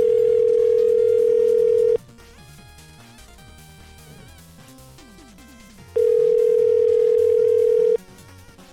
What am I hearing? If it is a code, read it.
Ringback tone (North America), 440 + 480 Hz.